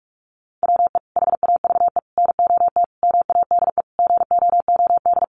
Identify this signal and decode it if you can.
Morse code, heard as a keyed tone, about 44 wpm, 700 Hz. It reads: WE HAVE NOT MADE GOOD
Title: Sentences from Top 1400 Words 45wpm